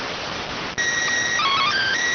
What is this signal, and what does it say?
SSTV (slow-scan television), PD 290.